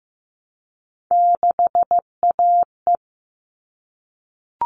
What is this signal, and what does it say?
Morse code, heard as a keyed tone, about 15 wpm, 700 Hz. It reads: 6AE
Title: All Cut Numbers - 3-Digit 15wpm